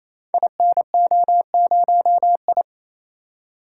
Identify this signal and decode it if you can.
Morse code, heard as a keyed tone, about 27 wpm, 700 Hz. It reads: INO0I
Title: Code Groups - 5 Letter:Numbers - 28wpm